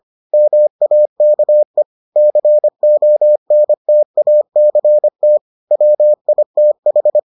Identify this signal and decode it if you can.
Morse code, heard as a keyed tone, about 25 wpm, 600 Hz. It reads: MAKE CONTACT WITH